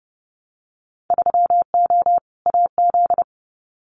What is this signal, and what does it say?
Morse code, heard as a keyed tone, about 30 wpm, 700 Hz. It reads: 3O AZ
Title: Winter Field Day Exchanges 30wpm